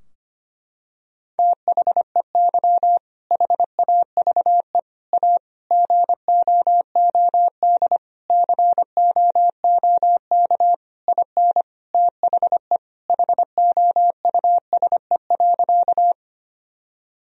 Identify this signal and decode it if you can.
Morse code, heard as a keyed tone, about 25 wpm, 700 Hz. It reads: THEY HAVE A GOOD COOK IN THE HOUSE.
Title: Sentences from Top 1000 Words - Review Part 4 - 25wpm